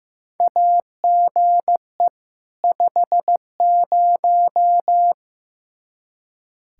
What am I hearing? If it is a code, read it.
Morse code, heard as a keyed tone, about 15 wpm, 700 Hz. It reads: AGE 50
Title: K1USN SST QSO Practice 15wpm